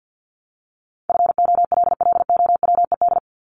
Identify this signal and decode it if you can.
Morse code, heard as a keyed tone, about 50 wpm, 700 Hz. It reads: FOLLOWED
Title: ED Suffix - Top 100 Words with Base Words 50wpm